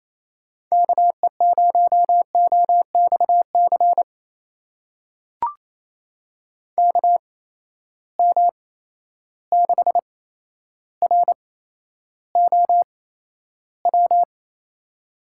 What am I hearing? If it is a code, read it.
Morse code, heard as a keyed tone, about 27 wpm, spaced as for 12 wpm, 700 Hz. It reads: KE0OXC KM6ROW
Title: USA Call Signs - 6 Characters - 8x Character Spacing - 28wpm